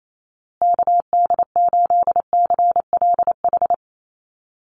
Morse code, heard as a keyed tone, about 28 wpm, 700 Hz. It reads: KD8CLH
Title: USA Call Signs Repeated 2x 28wpm